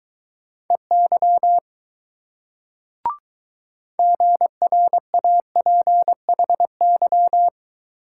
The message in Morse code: EY GRAPHY

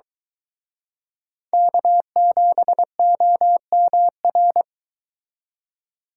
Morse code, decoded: K7OMR

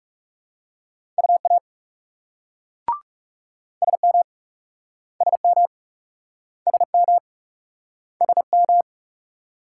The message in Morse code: UA SM SM SM SM

